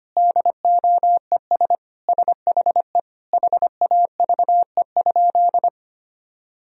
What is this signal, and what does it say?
Morse code, heard as a keyed tone, about 25 wpm, 700 Hz. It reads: DOES SHE HAVE?
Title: Sentences from Top 1000 Words - Review Part 1 - 25wpm